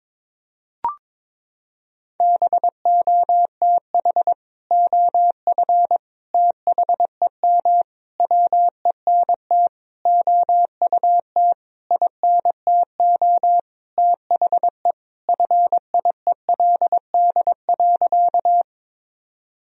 Morse code, heard as a keyed tone, about 22 wpm, 700 Hz. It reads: BOTH OF THEM WENT OUT INTO THE FIELD.
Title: Sentences from Top 1000 Words - Review Part 2 - 22wpm